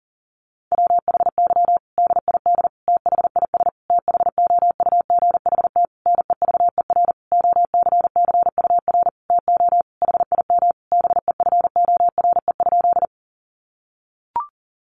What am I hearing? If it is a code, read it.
Morse code, heard as a keyed tone, about 40 wpm, 700 Hz. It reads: WHY DID THIS THOUGHT NEVER OCCUR TO HIM BEFORE?
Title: Sentences from Top 900 Words 40wpm